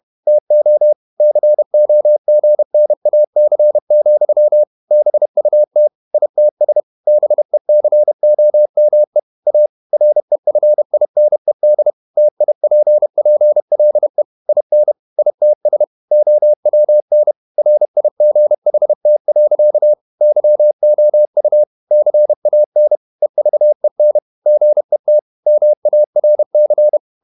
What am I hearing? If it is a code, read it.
Morse code, heard as a keyed tone, about 31 wpm, 600 Hz. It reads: TO COGNAC, BUT ITS BECOME A REFINED TIPPLE IN ITS OWN RIGHT. YOU CAN EVEN GET MARC
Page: Never 31wpm | CW med Gnister